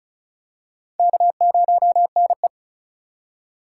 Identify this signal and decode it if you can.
Morse code, heard as a keyed tone, about 36 wpm, 700 Hz. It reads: K0NE
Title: USA Call Signs - 4 Characters - 3x Character Spacing - 35wpm